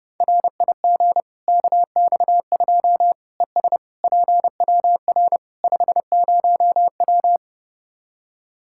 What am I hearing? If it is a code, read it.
Morse code, heard as a keyed tone, about 32 wpm, 700 Hz. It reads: RIG KX2 ES PWR 50W
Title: Practice QSO - Exchange 2 - Info - 30wpm